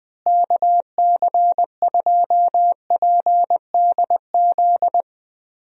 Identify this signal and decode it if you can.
Morse code, heard as a keyed tone, about 20 wpm, 700 Hz. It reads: KC2PDZ
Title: USA Call Signs - 6 Characters - Repeated 3x - 20wpm